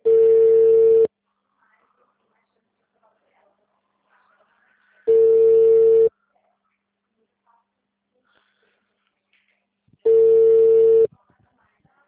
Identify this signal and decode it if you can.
Ringback tone (China, 450 Hz), with speech.
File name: out- -2003- - - .119